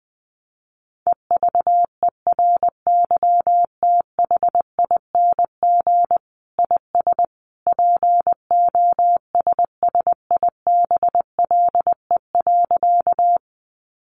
Morse code, heard as a keyed tone, about 19 wpm, 700 Hz. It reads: EVERYTHING IS POSSIBLE.
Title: Sentences from Top 700 Words 20wpm